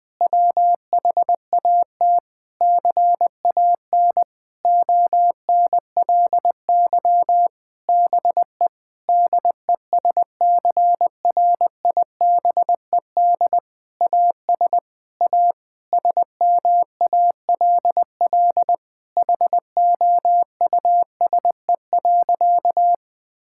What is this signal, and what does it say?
Morse code, heard as a keyed tone, about 20 wpm, 700 Hz. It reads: WHAT CAN ONLY BE DESCRIBED AS A SMALL HOUSE.
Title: Sentences with 8-Word Phrases - 20wpm